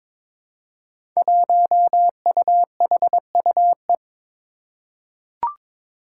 Morse code, heard as a keyed tone, about 22 wpm, 700 Hz. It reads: 1UHUE
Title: Five Challenging Letter-Number Combinations 22wpm